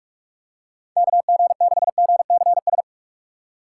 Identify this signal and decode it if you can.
Morse code, heard as a keyed tone, about 46 wpm, 700 Hz. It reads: KG6GCS
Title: USA Call Signs - 4x Character Spacing - 45wpm